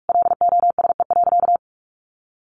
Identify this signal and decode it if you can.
Morse code, heard as a keyed tone, about 44 wpm, 700 Hz. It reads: LOSE.
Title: Sentences from Top 1100 Words 45wpm